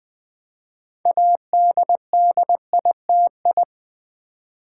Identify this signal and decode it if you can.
Morse code, heard as a keyed tone, about 20 wpm, 700 Hz. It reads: ADDITI
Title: Build A Word - 2 Letters At A Time - 8-Letter Words - 20wpm